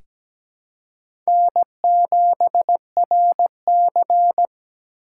Morse code, heard as a keyed tone, about 17 wpm, 700 Hz. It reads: N7RC
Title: USA Call Signs - 4 Characters - 2x Character Spacing - 17wpm